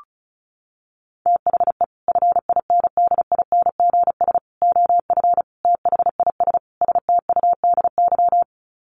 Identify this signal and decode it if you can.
Morse code, heard as a keyed tone, about 34 wpm, 700 Hz. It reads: THE FINDINGS OF THIS STUDY
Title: Sets of 5 Words Encore2 35wpm